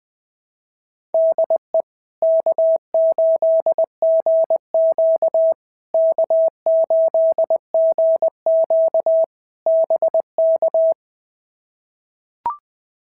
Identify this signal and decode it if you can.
Morse code, heard as a keyed tone, about 19 wpm, 650 Hz. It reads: DE K8GQ K8GQ BK
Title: POTA QSO Practice - 20wpm